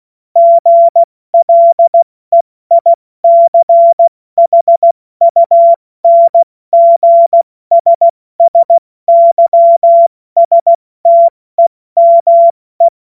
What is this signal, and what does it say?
Morse code, heard as a keyed tone, about 16 wpm, 680 Hz. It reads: GLEICHUNGSSYSTEME
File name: Morsekurs / Witze / Witz 21. Dezember 2021, 70 F80